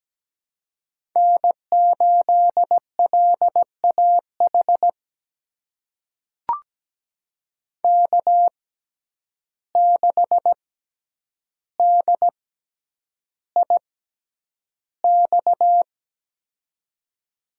Morse code, heard as a keyed tone, about 17 wpm, spaced as for 9 wpm, 700 Hz. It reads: N8LAH K6DIX
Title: USA Call Signs - 5 Characters - 6x Character Spacing - 17wpm